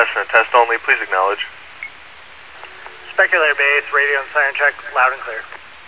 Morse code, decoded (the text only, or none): none